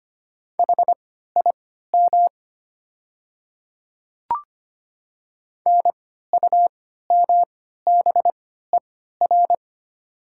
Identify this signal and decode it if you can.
Morse code, heard as a keyed tone, about 25 wpm, spaced as for 14 wpm, 700 Hz. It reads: HIM NUMBER
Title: 3x Character Spacing - Top 100 Words - 25wpm